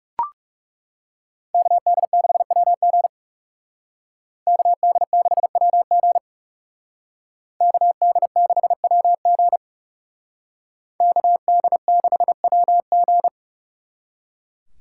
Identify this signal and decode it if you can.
Morse code, heard as a keyed tone, about 38 wpm, 700 Hz. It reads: KD6WG KD6WG KD6WG KD6WG